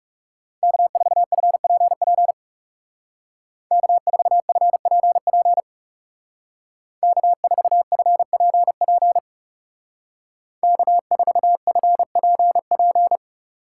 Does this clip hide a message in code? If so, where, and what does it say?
Yes: Morse code from 0.6 to 13.2 s, heard as a keyed tone, about 38 wpm, 700 Hz. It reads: K4FPP K4FPP K4FPP K4FPP